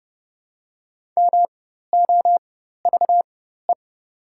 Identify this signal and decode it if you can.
Morse code, heard as a keyed tone, about 31 wpm, spaced as for 14 wpm, 700 Hz. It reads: MOVE
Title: 4x Character Spacing - Top 500 Words - 30wpm